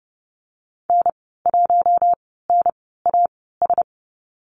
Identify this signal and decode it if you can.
Morse code, heard as a keyed tone, about 30 wpm, spaced as for 17 wpm, 700 Hz. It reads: N1NAS
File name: Lesson 14g - Number 2 - Call Signs 30wpm